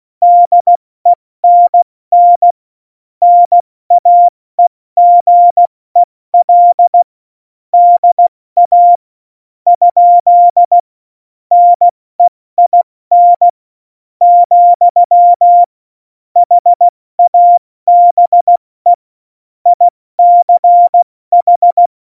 Morse code, heard as a keyed tone, about 16 wpm, spaced as for 13 wpm, 700 Hz. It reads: DENN NAEGEL DA? NEIN, HABE ICH
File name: Morsekurs / Witze / Witz 11. September 2018, 70 F80